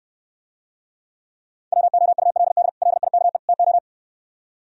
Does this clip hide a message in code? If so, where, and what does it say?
Yes: Morse code from 1.7 to 3.8 s, heard as a keyed tone, about 69 wpm, 700 Hz. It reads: WOULD LIKE TO